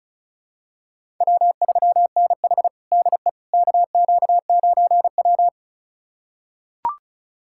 Morse code, heard as a keyed tone, about 34 wpm, 700 Hz. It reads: W3NH DE KQ9W
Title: USA Callsign DE Callsign - 4 Characters - 35wpm